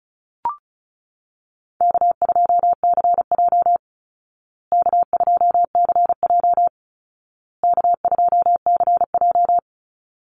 Morse code, heard as a keyed tone, about 34 wpm, 700 Hz. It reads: K2CJ K2CJ K2CJ
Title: Easy Call Signs Repeated 3x - 35wpm